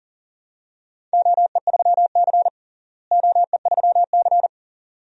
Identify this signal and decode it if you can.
Morse code, heard as a keyed tone, about 40 wpm, 700 Hz. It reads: OE3C OE3C
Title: DXCC-100 Call Signs - 4 Characters - Repeated 2x - 40wpm